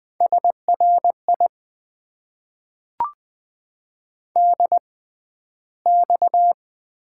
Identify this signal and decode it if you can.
Morse code, heard as a keyed tone, about 20 wpm, spaced as for 12 wpm, 700 Hz. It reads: SRI DX
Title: Top 90 QSO Elements - 6x Character Spacing - 20wpm